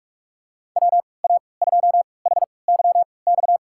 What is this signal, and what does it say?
Morse code, heard as a keyed tone, about 46 wpm, spaced as for 26 wpm, 700 Hz. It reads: WA2HYX
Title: USA Call Signs - 6 Characters - 3x Character Spacing - 45wpm